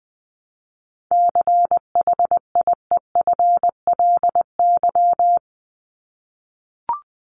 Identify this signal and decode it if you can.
Morse code, heard as a keyed tone, about 20 wpm, 700 Hz. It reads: CHIEFLY
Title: LY Suffix - Top 100 Words with Base Words 20wpm